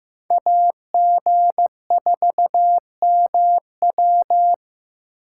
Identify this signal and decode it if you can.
Morse code, heard as a keyed tone, about 18 wpm, 700 Hz. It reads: AG4MW
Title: Speed Racing - USA 2x2 Call Signs 15wpm